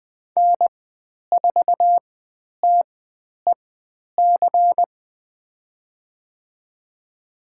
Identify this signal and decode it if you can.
Morse code, heard as a keyed tone, about 20 wpm, spaced as for 10 wpm, 700 Hz. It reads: N4TEC
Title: 109 Lesson 18g - Letter m - Call Signs 20wpm